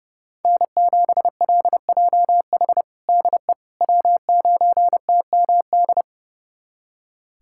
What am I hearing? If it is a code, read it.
Morse code, heard as a keyed tone, about 30 wpm, 700 Hz. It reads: N7LJH DE W9TMD
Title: USA Callsign DE Callsign - 5 Characters - 30wpm